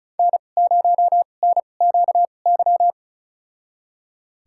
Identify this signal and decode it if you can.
Morse code, heard as a keyed tone, about 36 wpm, spaced as for 25 wpm, 700 Hz. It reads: N0NQY
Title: USA Call Signs - 5 Characters - 2x Character Spacing - 35wpm